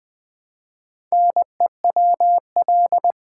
Morse code, heard as a keyed tone, about 20 wpm, 700 Hz. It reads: NEWL
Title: Build A Word - 1 Letter At A Time - 5-Letter Words - 20wpm